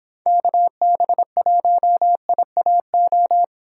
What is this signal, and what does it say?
Morse code, heard as a keyed tone, about 25 wpm, 700 Hz. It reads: KB1IAO